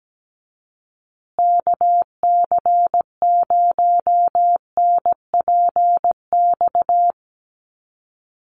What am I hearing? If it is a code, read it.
Morse code, heard as a keyed tone, about 17 wpm, 700 Hz. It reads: KC0NPX
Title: USA Call Signs Repeated 2x 17wpm